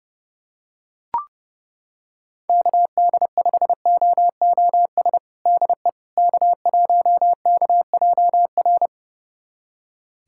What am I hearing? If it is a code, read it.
Morse code, heard as a keyed tone, about 30 wpm, 700 Hz. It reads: KD5OOS DE K1KJR
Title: USA Callsign DE Callsign 30wpm